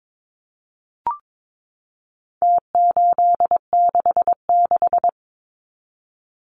Morse code, heard as a keyed tone, about 22 wpm, 700 Hz. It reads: T866
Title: Numbers - Contest Style - 4-Digit 22wpm